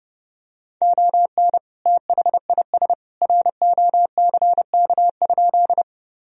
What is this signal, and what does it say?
Morse code, heard as a keyed tone, about 30 wpm, 700 Hz. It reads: ON THIS ROCK?
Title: Sentences from Top 1000 Words - Review Part 2 - 30wpm